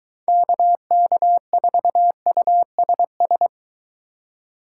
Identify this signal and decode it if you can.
Morse code, heard as a keyed tone, about 23 wpm, 700 Hz. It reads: KK4USS